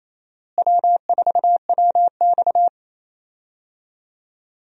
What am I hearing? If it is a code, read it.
Morse code, heard as a keyed tone, about 28 wpm, 700 Hz. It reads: W4WX
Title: USA Call Signs - 4 Characters - Repeated 3x - 28wpm